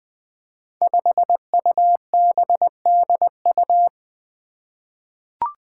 Morse code, heard as a keyed tone, about 20 wpm, 700 Hz. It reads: 5UBDU
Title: Five Challenging Letter-Number Combinations 20wpm